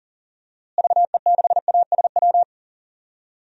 Morse code, heard as a keyed tone, about 37 wpm, 700 Hz. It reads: VE6ASW
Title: DXCC-100 Call Signs - 6 Characters - Repeated 2x - 40wpm